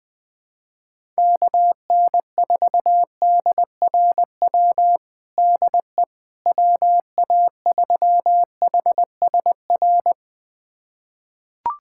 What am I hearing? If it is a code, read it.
Morse code, heard as a keyed tone, about 20 wpm, 700 Hz. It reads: KN4DRW DE WA3HSR
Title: USA Callsign DE Callsign 20wpm